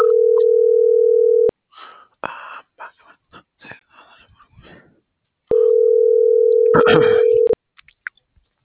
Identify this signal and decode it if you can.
Ringback tone (North America), 440 + 480 Hz.